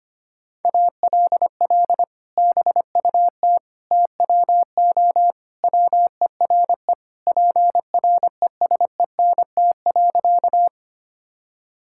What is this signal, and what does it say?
Morse code, heard as a keyed tone, about 25 wpm, 700 Hz. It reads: ALL BUT TWO WERE PRESENT.
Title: Sentences from Top 1000 Words - Review Part 5 - 25wpm